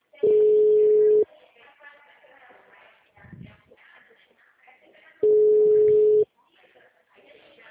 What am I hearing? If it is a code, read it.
Ringback tone (Europe, 425 Hz), with speech.